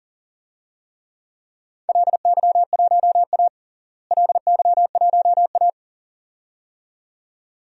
Morse code, heard as a keyed tone, about 42 wpm, 690 Hz. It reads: LY1A LY1A
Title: DXCC-100 Call Signs - 4 Characters - Repeated 2x - 40wpm